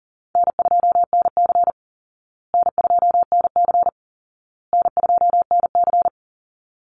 Morse code, heard as a keyed tone, about 40 wpm, 700 Hz. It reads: N2NC N2NC N2NC
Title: Easy Call Signs Repeated 3x - 40wpm